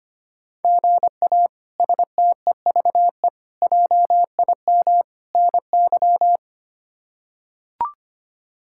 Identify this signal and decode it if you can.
Morse code, heard as a keyed tone, about 25 wpm, 700 Hz. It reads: GA STEVE JIM NY
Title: K1USN SST QSO Practice - 25wpm